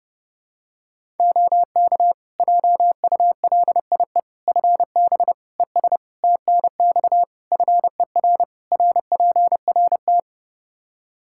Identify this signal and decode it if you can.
Morse code, heard as a keyed tone, about 30 wpm, 700 Hz. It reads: OK JULIE FB ES TNX FER RPRT
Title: Practice QSO - Exchange 2 - Info - 30wpm